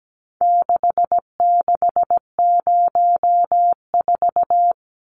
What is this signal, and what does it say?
Morse code, heard as a keyed tone, about 17 wpm, 700 Hz. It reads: 6604
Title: Numbers - 4-Digit 17wpm